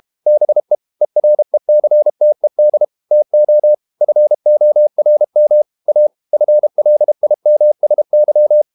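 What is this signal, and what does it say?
Morse code, heard as a keyed tone, about 32 wpm, 600 Hz. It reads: BE ERECTED TO FORM A FLIMSY
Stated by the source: Morse code practice at 32 wpm from CW med Gnister, reading Ken Follett's novel Never in English.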